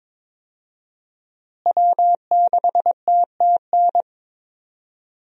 Morse code, heard as a keyed tone, about 22 wpm, 700 Hz. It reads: W6TTN
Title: USA Call Signs - 5 Characters - Repeated 2x - 22wpm